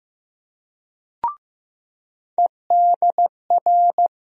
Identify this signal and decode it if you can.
Morse code, heard as a keyed tone, about 15 wpm, 700 Hz. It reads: EDR
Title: USA Call Sign Postfixes - 3 Letters 15wpm